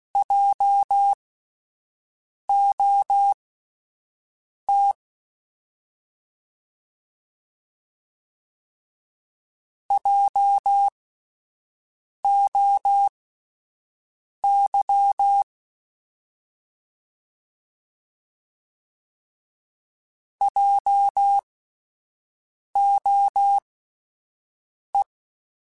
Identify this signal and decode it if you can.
Morse code, heard as a keyed tone, about 16 wpm, spaced as for 5 wpm, 800 Hz. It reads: JOT JOY JOE